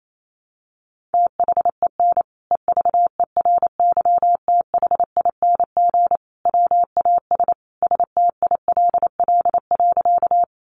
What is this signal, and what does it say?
Morse code, heard as a keyed tone, about 28 wpm, 700 Hz. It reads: THEN EVERYTHING WAS STILL.
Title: Sentences from Top 700 Words 28wpm